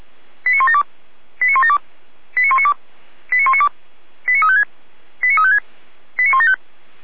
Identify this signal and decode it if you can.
Five-tone selective call (ZVEI), heard as quick runs of five tones, tones of about 70 ms. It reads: 89172 89172 89192 89192 89356 89356 89175
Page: feuerwehr-orlamuende